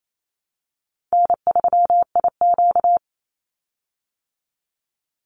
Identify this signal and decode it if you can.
Morse code, heard as a keyed tone, about 28 wpm, 700 Hz. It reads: N3IQ
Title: Easy Call Signs Repeated 2x - 28wpm